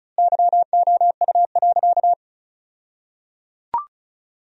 Morse code, heard as a keyed tone, about 36 wpm, 700 Hz. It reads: YOU.
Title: 2x Word-Spacing - Sentences from Top 200 Words - 35wpm